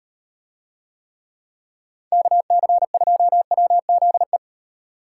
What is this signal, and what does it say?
Morse code, heard as a keyed tone, about 38 wpm, 700 Hz. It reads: KC2WZE